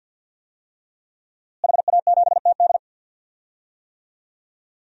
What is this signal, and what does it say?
Morse code, heard as a keyed tone, about 51 wpm, 700 Hz. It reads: HA7TD
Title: DXCC-100 Call Signs - 5 Characters - Repeated 3x - 50wpm